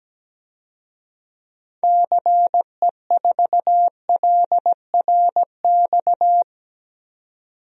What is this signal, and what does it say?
Morse code, heard as a keyed tone, about 17 wpm, 700 Hz. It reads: CE4LRX
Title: DXCC-100 Call Signs - 6 Characters - 17wpm